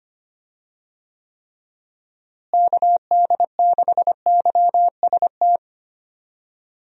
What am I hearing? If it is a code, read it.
Morse code, heard as a keyed tone, about 25 wpm, 700 Hz. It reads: KD6YST
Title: USA Call Signs - 6 Characters - Repeated 3x - 25wpm